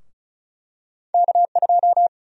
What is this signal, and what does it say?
Morse code, heard as a keyed tone, about 34 wpm, 700 Hz. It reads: K2
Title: USA Call Sign Prefixes - Repeated 2x - 35wpm